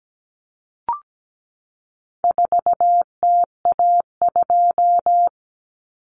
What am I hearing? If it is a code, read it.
Morse code, heard as a keyed tone, about 17 wpm, 700 Hz. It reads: 4TA2
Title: Common Cut Numbers - 4-Digit 17wpm